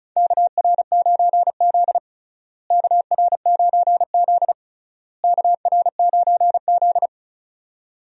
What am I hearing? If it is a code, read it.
Morse code, heard as a keyed tone, about 36 wpm, 700 Hz. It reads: KR9Z KR9Z KR9Z
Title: USA Call Signs - 4 Characters - Repeated 3x - 35wpm